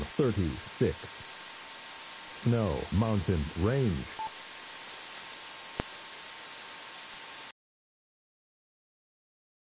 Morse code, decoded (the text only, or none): none